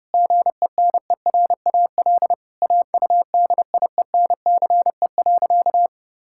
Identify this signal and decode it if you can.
Morse code, heard as a keyed tone, about 30 wpm, 700 Hz. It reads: GENERAL AUDIENCE.
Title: Sentences from Top 1000 Words - Review Part 5 - 30wpm